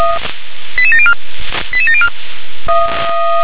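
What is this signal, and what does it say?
Five-tone selective call (ZVEI), heard as quick runs of five tones, tones of about 70 ms. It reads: 88704 88704
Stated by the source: ZVEI five-tone selective call received on SDR.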